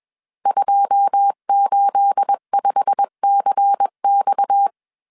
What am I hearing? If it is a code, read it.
Morse code, heard as a keyed tone, about 21 wpm, 800 Hz. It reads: 285CX
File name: CW / DET / 1 9 1500C S C 20wpm 2310